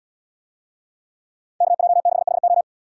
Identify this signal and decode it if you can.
Morse code, heard as a keyed tone, about 69 wpm, 700 Hz. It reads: KJ6HO